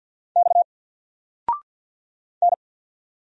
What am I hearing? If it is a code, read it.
Morse code, heard as a keyed tone, about 48 wpm, 700 Hz. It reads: X N
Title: Single Character (Letter:Number:Punctuation:Prosign) 50wpm